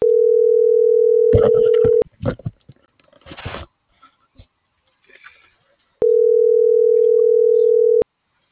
Ringback tone (North America), 440 + 480 Hz.